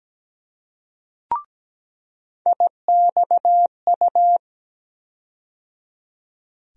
Morse code, heard as a keyed tone, about 17 wpm, 700 Hz. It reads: IXU
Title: USA Call Sign Postfixes - 3 Letters 17wpm